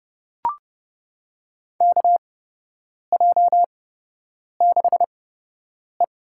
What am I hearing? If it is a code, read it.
Morse code, heard as a keyed tone, about 30 wpm, spaced as for 8 wpm, 700 Hz. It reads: KJ6E
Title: USA Call Signs - 8x Character Spacing - 30wpm